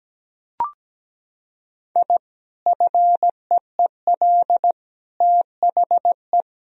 Morse code, heard as a keyed tone, about 17 wpm, 700 Hz. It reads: I FEEL THE
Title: Sentences from Top 1000 Words - Review Part 3 - 17wpm